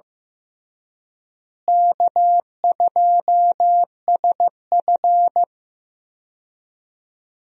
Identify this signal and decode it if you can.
Morse code, heard as a keyed tone, about 15 wpm, 700 Hz. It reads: K2SF
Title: USA Call Signs - 4 Characters - Repeated 3x - 15wpm